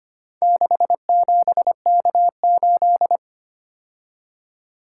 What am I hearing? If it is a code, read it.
Morse code, heard as a keyed tone, about 25 wpm, 700 Hz. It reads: 67K8
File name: practice / 25wpm / Four Letter-Number Combinations - 25wpm